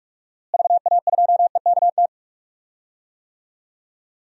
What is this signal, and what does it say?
Morse code, heard as a keyed tone, about 47 wpm, 700 Hz. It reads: VA2EKT
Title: DXCC-100 Call Signs - 6 Characters - Repeated 2x - 45wpm